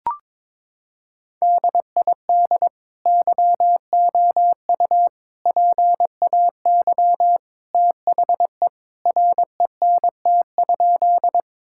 Morse code, heard as a keyed tone, about 22 wpm, 700 Hz. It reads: DID YOU PAY THE RENT?
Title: Sentences from Top 1000 Words - Review Part 5 - 22wpm